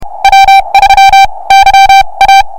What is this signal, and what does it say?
Morse code, heard as a keyed tone, about 32 wpm, spaced as for 25 wpm, 2350 Hz, with noise. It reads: W3YA